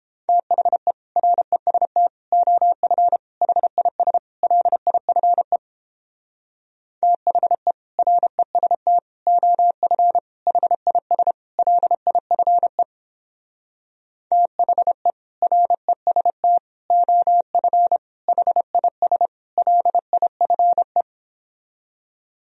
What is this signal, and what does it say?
Morse code, heard as a keyed tone, about 29 wpm, 700 Hz. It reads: THE REST OF HIS LIFE THE REST OF HIS LIFE THE REST OF HIS LIFE